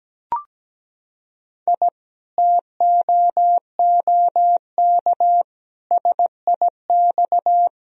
Morse code, heard as a keyed tone, about 17 wpm, 700 Hz. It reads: I TOOK SIX
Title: Sentences from Top 1000 Words - Review Part 2 - 17wpm